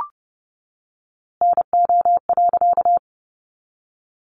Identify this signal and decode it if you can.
Morse code, heard as a keyed tone, about 30 wpm, 700 Hz. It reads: NO.
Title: A Sound Of Thunder 30wpm